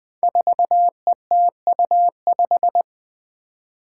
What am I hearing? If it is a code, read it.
Morse code, heard as a keyed tone, about 20 wpm, 700 Hz. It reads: 4ETU5
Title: Five Challenging Letter-Number Combinations 20wpm